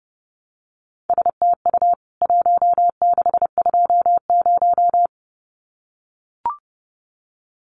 Morse code, heard as a keyed tone, about 30 wpm, 700 Hz. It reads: STU 1620
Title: CWT Exchange 30wpm